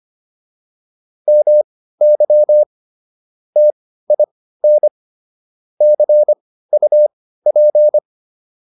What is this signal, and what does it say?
Morse code, heard as a keyed tone, about 25 wpm, spaced as for 15 wpm, 600 Hz. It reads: MY TIN CUP